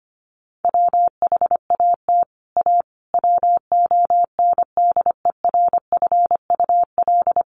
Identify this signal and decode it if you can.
Morse code, heard as a keyed tone, about 25 wpm, 700 Hz. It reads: WHAT A WONDERFUL
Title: Sentences from Top 1100 Words 25wpm